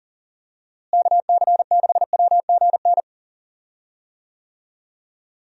Morse code, heard as a keyed tone, about 37 wpm, 700 Hz. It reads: KC6WGN
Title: USA Call Signs - 6 Characters - Repeated 2x - 40wpm